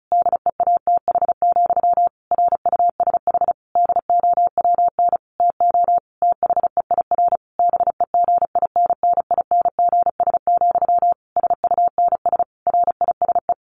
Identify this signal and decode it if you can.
Morse code, heard as a keyed tone, about 34 wpm, 700 Hz. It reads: DEATH, RUSH DOWN TO THEIR BEGINNINGS, SUNS RISE
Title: A Sound Of Thunder 35wpm